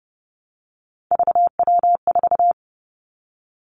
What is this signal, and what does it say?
Morse code, heard as a keyed tone, about 30 wpm, 700 Hz. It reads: VW4
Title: Three Challenging Letter-Number Combinations 30wpm